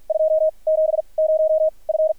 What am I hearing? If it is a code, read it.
Morse code, heard as a keyed tone, about 50 wpm, spaced as for 31 wpm, 630 Hz. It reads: 270F
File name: archived / CRSnet